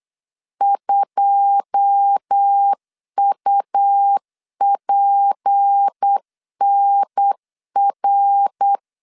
Morse code, heard as a keyed tone, about 8 wpm, 800 Hz. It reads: 2UPNR
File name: LEZIONI / 9 28 8wpm 3512